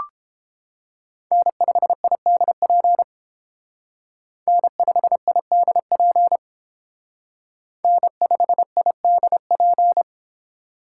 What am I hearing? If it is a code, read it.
Morse code, heard as a keyed tone, about 29 wpm, 700 Hz. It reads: N5IDP N5IDP N5IDP